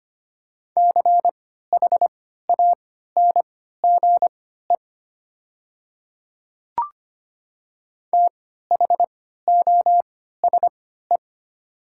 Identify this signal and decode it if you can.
Morse code, heard as a keyed tone, about 25 wpm, spaced as for 14 wpm, 700 Hz. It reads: CHANGE THOSE
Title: 3x Character Spacing - Top 300 Words - 25wpm